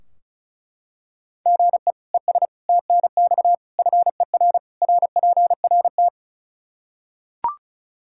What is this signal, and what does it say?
Morse code, heard as a keyed tone, about 34 wpm, 700 Hz. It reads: GE ES TNX FER RPRT
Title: K1USN SST QSO Practice 35wpm